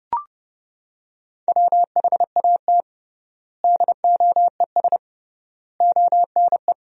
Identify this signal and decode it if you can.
Morse code, heard as a keyed tone, about 30 wpm, 700 Hz. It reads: WHAT DOES ONE
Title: 3x Word-Spacing - Sentences from Top 200 Words - 30wpm